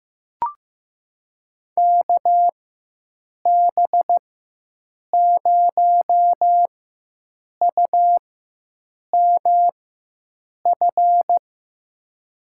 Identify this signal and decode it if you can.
Morse code, heard as a keyed tone, about 15 wpm, spaced as for 7 wpm, 700 Hz. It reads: KB0UMF
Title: USA Call Signs - 6 Characters - 4x Character Spacing - 15wpm